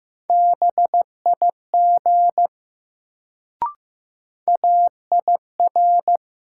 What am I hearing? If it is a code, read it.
Morse code, heard as a keyed tone, about 15 wpm, 700 Hz. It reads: BIG AIR
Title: Sentences from Top 200 Words - New Word Drill - 15wpm